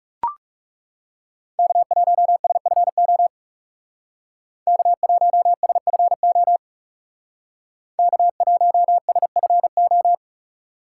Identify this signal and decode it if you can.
Morse code, heard as a keyed tone, about 38 wpm, 700 Hz. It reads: K1SFO K1SFO K1SFO